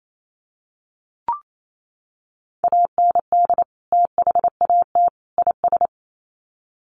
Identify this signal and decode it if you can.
Morse code, heard as a keyed tone, about 28 wpm, 700 Hz. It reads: AND THAT IS
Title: Sets of 3 Words 28wpm